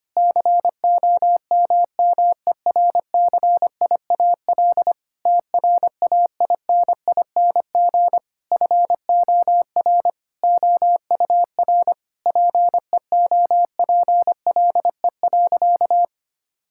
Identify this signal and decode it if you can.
Morse code, heard as a keyed tone, about 25 wpm, 700 Hz. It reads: COMMERCIAL TRAINING FOR OUR PEOPLE.
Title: Sentences from Top 1000 Words - Review Part 5 - 25wpm